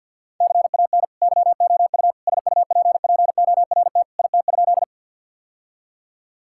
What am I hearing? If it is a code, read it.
Morse code, heard as a keyed tone, about 52 wpm, 700 Hz. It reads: CAN YOU SUPPORT IT?
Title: Sentences from Top 1000 Words - Review Part 5 - 50wpm